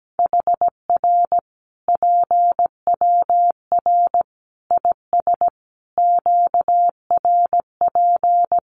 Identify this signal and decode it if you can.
Morse code, heard as a keyed tone, about 17 wpm, 700 Hz. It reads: HR PWR IS QRP
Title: Common QSO Phrases 17wpm